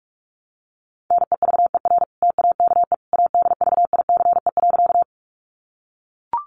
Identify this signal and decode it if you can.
Morse code, heard as a keyed tone, about 44 wpm, 700 Hz. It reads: NEVER TAKE ADVICE.
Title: Sentences from Top 1100 Words 45wpm